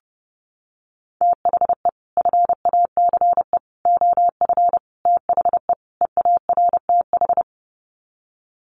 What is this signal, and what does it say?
Morse code, heard as a keyed tone, about 30 wpm, 700 Hz. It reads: THE FACE OF THE EARTH
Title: Sets of 5 Words Encore 30wpm